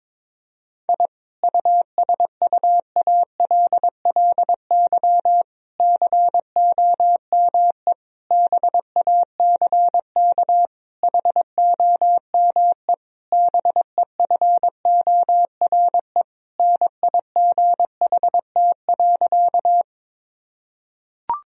Morse code, heard as a keyed tone, about 22 wpm, 700 Hz. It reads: I USUALLY COME BACK HOME BEFORE NIGHT.
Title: Sentences from Top 1000 Words - Review Part 3 - 22wpm